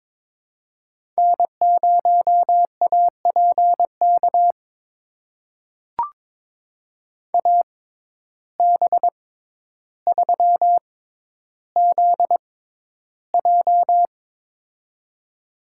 Morse code, heard as a keyed tone, about 21 wpm, spaced as for 11 wpm, 700 Hz. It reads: N0APK AB3ZJ
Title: USA Call Signs - 5 Characters - 6x Character Spacing - 22wpm